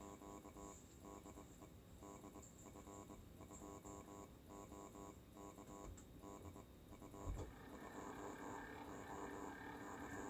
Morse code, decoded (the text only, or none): none